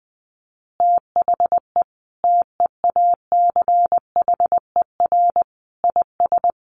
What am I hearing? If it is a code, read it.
Morse code, heard as a keyed tone, about 20 wpm, 700 Hz. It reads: THE TEACHER IS
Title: Easy Phrases 20wpm